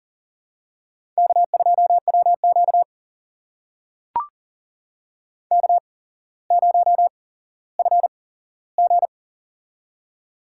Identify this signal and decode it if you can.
Morse code, heard as a keyed tone, about 40 wpm, spaced as for 17 wpm, 700 Hz. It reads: K2WQ K0FG
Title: USA Call Signs - 8x Character Spacing - 40wpm